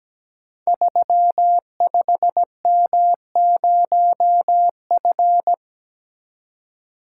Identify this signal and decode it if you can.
Morse code, heard as a keyed tone, about 17 wpm, 700 Hz. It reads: 35M0F
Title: Code Groups - 5 Letter:Numbers - 17wpm